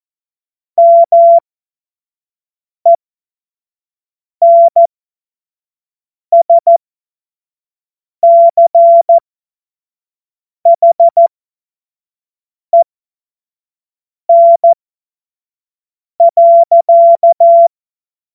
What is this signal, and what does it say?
Morse code, heard as a keyed tone, about 14 wpm, spaced as for 5 wpm, 680 Hz. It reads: MENSCHEN.
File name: Witz-15-Juli-2025-25-F70